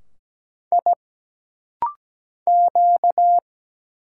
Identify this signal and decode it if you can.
Morse code, heard as a keyed tone, about 17 wpm, 700 Hz. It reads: I Q
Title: Single Letter-Number Alt - Rapid-Fire 17wpm